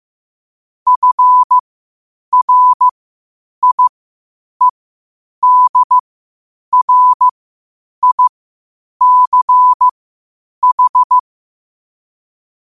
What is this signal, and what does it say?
Morse code, heard as a keyed tone, about 15 wpm, spaced as for 8 wpm, 1000 Hz. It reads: FRIEDRICH
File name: qotdfiles / 8wpm